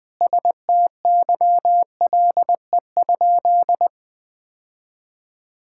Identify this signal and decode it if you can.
Morse code, heard as a keyed tone, about 21 wpm, 700 Hz. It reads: STYLE?
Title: Sentences from Top 1000 Words - Review Part 5 - 20wpm